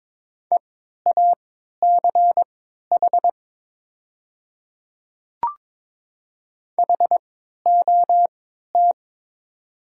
Morse code, heard as a keyed tone, about 22 wpm, spaced as for 13 wpm, 700 Hz. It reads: EACH HOT
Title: 3x Character Spacing - Top 100 Words - 22wpm